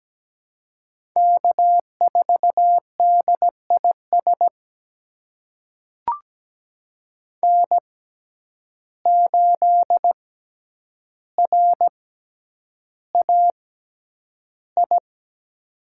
Morse code, heard as a keyed tone, about 17 wpm, spaced as for 9 wpm, 700 Hz. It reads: K4DIS N8RAI
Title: USA Call Signs - 5 Characters - 6x Character Spacing - 17wpm